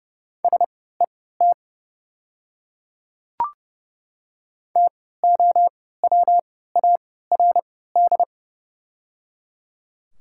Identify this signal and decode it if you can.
Morse code, heard as a keyed tone, about 30 wpm, spaced as for 17 wpm, 700 Hz. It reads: SET TOWARD
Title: 3x Character Spacing - Top 500 Words - 30wpm